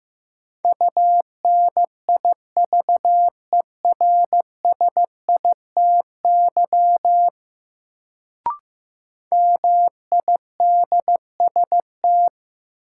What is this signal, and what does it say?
Morse code, heard as a keyed tone, about 16 wpm, 700 Hz. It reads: UNIVERSITY MIDST
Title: Sets of 4 Words - New Word Drill - 15wpm